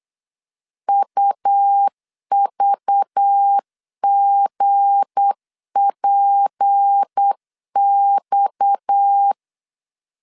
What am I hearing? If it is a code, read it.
Morse code, heard as a keyed tone, about 8 wpm, 800 Hz. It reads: UVGPX
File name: LEZIONI / 8 YJ45 / 8 YJ 8wpm 3313